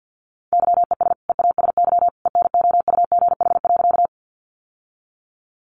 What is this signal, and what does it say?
Morse code, heard as a keyed tone, about 50 wpm, 700 Hz. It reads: YES EASY ENOUGH.
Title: Sentences from Top 700 Words 50wpm